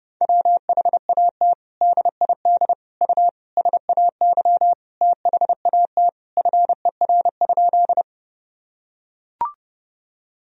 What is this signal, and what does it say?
Morse code, heard as a keyed tone, about 30 wpm, 700 Hz. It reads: WHAT DID U SAY THAT FER?
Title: Sentences from Top 200 Words and Abbreviations 30wpm